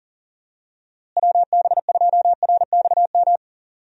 Morse code, heard as a keyed tone, about 39 wpm, 700 Hz. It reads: WB2RXM
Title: USA Call Signs - 6 Characters - Repeated 2x - 40wpm